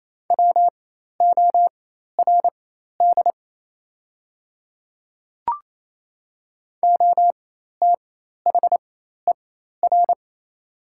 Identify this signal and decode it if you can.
Morse code, heard as a keyed tone, about 28 wpm, spaced as for 13 wpm, 700 Hz. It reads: WORD OTHER
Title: 4x Character Spacing - Top 100 Words - 28wpm